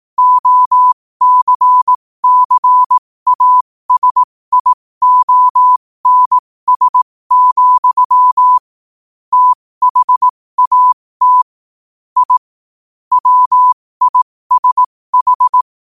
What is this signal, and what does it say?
Morse code, heard as a keyed tone, about 18 wpm, spaced as for 15 wpm, 1000 Hz. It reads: OCCASIONS, THAT I WISH
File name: qotdfiles / 15wpm